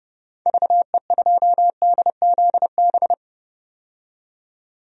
Morse code, heard as a keyed tone, about 32 wpm, 700 Hz. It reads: VE2DZB
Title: DXCC-100 Call Signs - 6 Characters - 30wpm